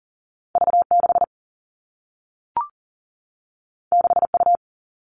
Morse code, heard as a keyed tone, about 42 wpm, 700 Hz. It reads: V 6 6 U
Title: Two Challenging Letter-Number Combinations 40wpm